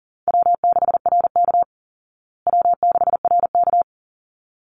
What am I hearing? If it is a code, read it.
Morse code, heard as a keyed tone, about 40 wpm, 700 Hz. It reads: W6RK W6RK
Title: Easy Call Signs Repeated 2x - 40wpm